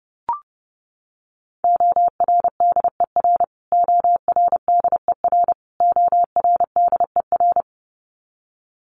Morse code, heard as a keyed tone, about 30 wpm, 700 Hz. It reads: ORDER ORDER ORDER
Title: Top 500 Words Repeated 30wpm